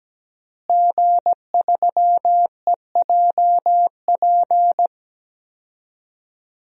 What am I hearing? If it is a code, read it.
Morse code, heard as a keyed tone, about 17 wpm, 700 Hz. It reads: G3EJP
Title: DXCC-100 Call Signs - 5 Characters - Repeated 3x - 17wpm